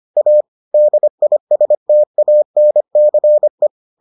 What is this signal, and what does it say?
Morse code, heard as a keyed tone, about 25 wpm, 600 Hz. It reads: A DISTANCE